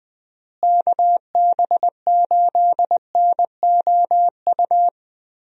Morse code, heard as a keyed tone, about 21 wpm, 700 Hz. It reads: KB8NOU
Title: USA Call Signs - 6 Characters - Repeated 2x - 20wpm